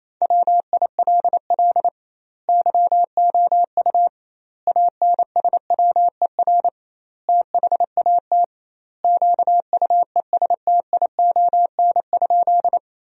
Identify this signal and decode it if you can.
Morse code, heard as a keyed tone, about 28 wpm, 700 Hz. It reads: WILL YOU ANSWER THAT QUESTION?
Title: 2x Word-Spacing - Sentences from Top 400 Words - 28wpm